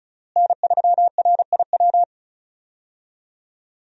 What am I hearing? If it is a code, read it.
Morse code, heard as a keyed tone, about 34 wpm, 700 Hz. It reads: N3RIW
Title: USA Call Signs - 5 Characters - Repeated 3x - 35wpm